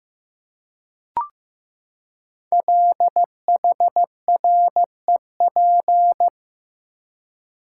Morse code, heard as a keyed tone, about 15 wpm, 700 Hz. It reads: LHREP
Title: Code Groups - 5 Letters - 15wpm